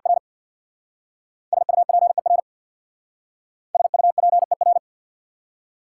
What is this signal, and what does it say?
Morse code, heard as a keyed tone, about 59 wpm, 700 Hz. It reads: R SUPER SUPER
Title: Speed-Racing Prefixes and Suffixes 50wpm